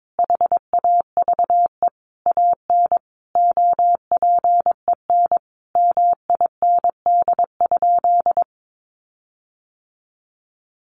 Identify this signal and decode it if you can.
Morse code, heard as a keyed tone, about 22 wpm, 700 Hz. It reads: HAVE AN OPEN MIND?
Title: Sentences from Top 1300 Words 22wpm